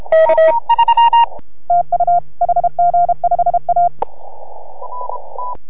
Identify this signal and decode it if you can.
Morse code, heard as a keyed tone, about 29 wpm, 690 Hz. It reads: K TU HG5A